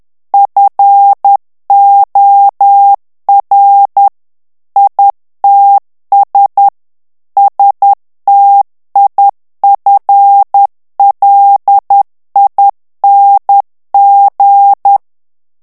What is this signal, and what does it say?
Morse code, heard as a keyed tone, about 10 wpm, 800 Hz. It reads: FOR ITS STIFLING
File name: ANGLAIS texte clair 2 vitesse 10 mots minute